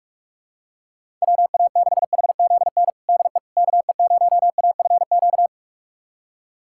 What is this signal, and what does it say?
Morse code, heard as a keyed tone, about 46 wpm, 700 Hz. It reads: WA6HZN DE KE0AFQ
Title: USA Callsign DE Callsign - 6 Characters - 45wpm